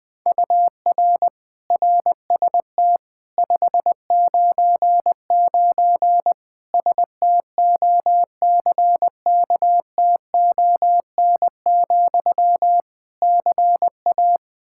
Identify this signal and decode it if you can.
Morse code, heard as a keyed tone, about 21 wpm, 700 Hz. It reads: UR RST 599 STOCKTON, CA
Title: K1USN SST QSO Practice 20wpm